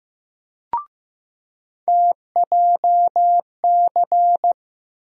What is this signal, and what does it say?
Morse code, heard as a keyed tone, about 15 wpm, 700 Hz. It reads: TJC
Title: USA Call Sign Postfixes - 3 Letters 15wpm